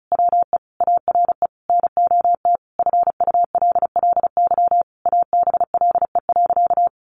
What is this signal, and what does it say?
Morse code, heard as a keyed tone, about 34 wpm, 700 Hz. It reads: WE ARE NOT FULLY ABLE.
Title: Sentences from Top 1300 Words 35wpm